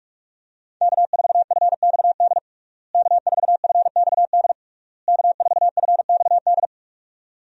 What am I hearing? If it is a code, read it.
Morse code, heard as a keyed tone, about 46 wpm, 700 Hz. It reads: K4FXD K4FXD K4FXD
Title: USA Call Signs - 5 Characters - Repeated 3x - 45wpm